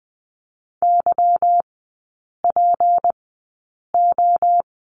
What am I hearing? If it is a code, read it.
Morse code, heard as a keyed tone, about 19 wpm, spaced as for 8 wpm, 700 Hz. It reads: YPO
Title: 191 Lesson 31e - Letter v - Three Characters 20wpm